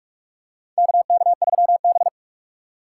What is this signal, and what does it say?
Morse code, heard as a keyed tone, about 43 wpm, 690 Hz. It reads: KK3B
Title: USA Call Signs - 4 Characters - 3x Character Spacing - 45wpm